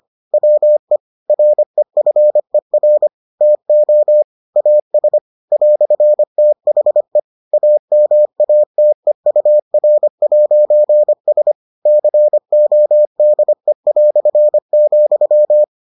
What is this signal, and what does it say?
Morse code, heard as a keyed tone, about 25 wpm, 600 Hz. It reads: WE REFER TO AS "THE AMATEUR'S CODE",